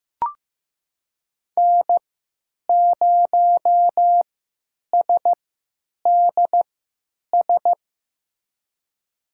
Morse code, heard as a keyed tone, about 15 wpm, spaced as for 9 wpm, 700 Hz. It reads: N0SDS
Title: USA Call Signs - 5 Characters - 3x Character Spacing - 15wpm